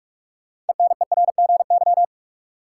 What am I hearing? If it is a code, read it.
Morse code, heard as a keyed tone, about 46 wpm, 700 Hz. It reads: ENERGY
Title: Sentences from Top 900 Words - New Word Drill - 45wpm